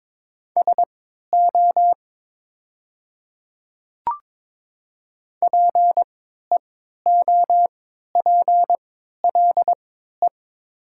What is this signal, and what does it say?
Morse code, heard as a keyed tone, about 22 wpm, spaced as for 12 wpm, 700 Hz. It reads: SO PEOPLE
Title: 3x Character Spacing - Top 100 Words - 22wpm